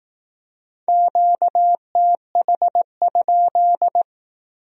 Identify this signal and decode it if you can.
Morse code, heard as a keyed tone, about 18 wpm, 700 Hz. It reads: QTH?
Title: Speed-Racing Q-Codes - 15wpm